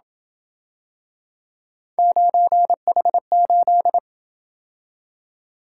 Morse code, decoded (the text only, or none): 9H8